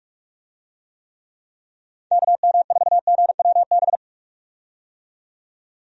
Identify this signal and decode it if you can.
Morse code, heard as a keyed tone, about 44 wpm, 700 Hz. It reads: KM4GWB
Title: USA Call Signs - 6 Characters - 45wpm